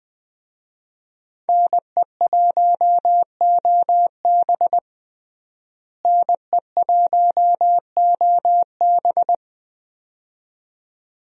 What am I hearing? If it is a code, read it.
Morse code, heard as a keyed tone, about 21 wpm, 700 Hz. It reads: NE1OB NE1OB
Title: USA Call Signs - 5 Characters - Repeated 2x - 20wpm